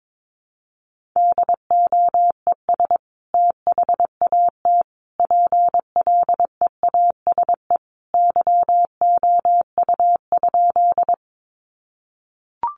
Morse code, heard as a keyed tone, about 22 wpm, 700 Hz. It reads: DOES THAT PLEASE YOU?
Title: Sentences from Top 1500 Words 22wpm